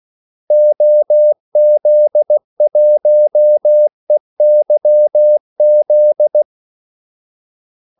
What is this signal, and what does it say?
Morse code, heard as a keyed tone, about 16 wpm, 600 Hz. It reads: OZ1EYZ
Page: Kaldesignaler 16 wpm